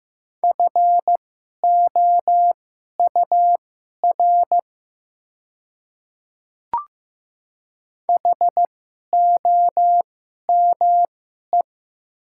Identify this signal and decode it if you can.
Morse code, heard as a keyed tone, about 15 wpm, spaced as for 11 wpm, 700 Hz. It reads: FOUR HOME
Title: 2x Character Spacing - Top 300 Words - 15wpm